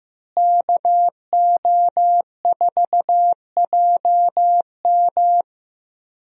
Morse code, heard as a keyed tone, about 15 wpm, 700 Hz. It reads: KO4JM
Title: USA Call Signs - 5 Characters - Repeated 2x - 15wpm